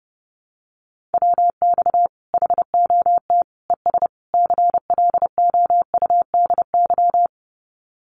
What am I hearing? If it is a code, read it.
Morse code, heard as a keyed tone, about 30 wpm, 700 Hz. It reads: WX HOT ES CLOUDY
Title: Common QSO Phrases 30wpm